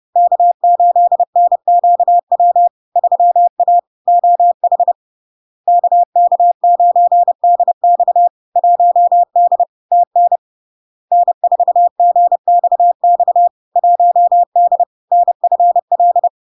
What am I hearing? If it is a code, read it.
Morse code, heard as a keyed tone, about 30 wpm, 700 Hz. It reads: K8NQW 3A OH KK9DX 1D TN N4GXX 1D NFL
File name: LARK-FD-2022-Log-in-CW-30wpm-700-Hz